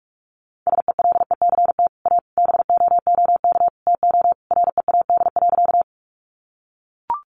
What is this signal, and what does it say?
Morse code, heard as a keyed tone, about 44 wpm, 700 Hz. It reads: SELECT A BOOK TO READ.
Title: Sentences from Top 1300 Words 45wpm